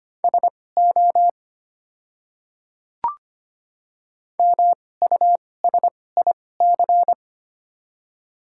Morse code, heard as a keyed tone, about 25 wpm, spaced as for 18 wpm, 690 Hz. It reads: SO MUSIC
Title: 2x Character Spacing - Top 300 Words - 25wpm